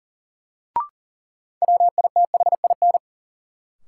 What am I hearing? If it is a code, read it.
Morse code, heard as a keyed tone, about 40 wpm, 700 Hz. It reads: WITHIN
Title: Sentences from Top 700 Words - New Word Drill - 40wpm